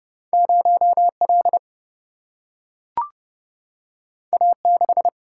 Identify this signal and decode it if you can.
Morse code, heard as a keyed tone, about 30 wpm, 700 Hz. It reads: 0L A6
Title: Single-Letter Single-Number 30wpm